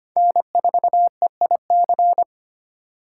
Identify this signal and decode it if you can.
Morse code, heard as a keyed tone, about 25 wpm, 700 Hz. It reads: N4EIC
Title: USA Call Signs - 5 Characters - 2x Character Spacing - 25wpm